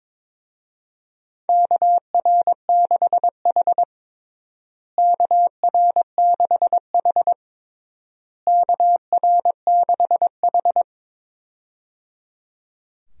Morse code, heard as a keyed tone, about 22 wpm, 700 Hz. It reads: KR6H KR6H KR6H
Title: USA Call Signs - 4 Characters - Repeated 3x - 22wpm